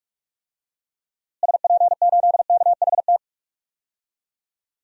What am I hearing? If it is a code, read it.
Morse code, heard as a keyed tone, about 46 wpm, 700 Hz. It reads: SP8KHT
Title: DXCC-100 Call Signs - 6 Characters - Repeated 2x - 45wpm